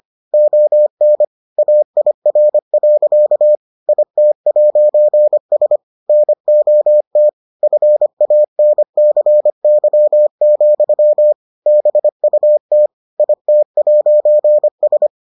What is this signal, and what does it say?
Morse code, heard as a keyed tone, about 25 wpm, 600 Hz. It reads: ON AIR. IT'S NOT FANCY, BUT IT'S